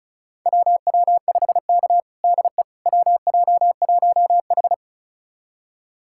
Morse code, heard as a keyed tone, about 36 wpm, 700 Hz. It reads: WW5K DE WJ1H
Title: USA Callsign DE Callsign - 4 Characters - 35wpm